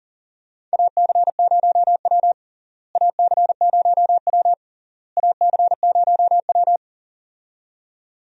Morse code, heard as a keyed tone, about 37 wpm, 700 Hz. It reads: AC0W AC0W AC0W
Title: USA Call Signs - 4 Characters - Repeated 3x - 40wpm